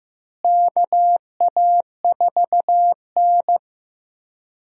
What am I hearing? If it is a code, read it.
Morse code, heard as a keyed tone, about 15 wpm, 700 Hz. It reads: KA4N
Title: USA Call Signs - 4 Characters - Repeated 2x - 15wpm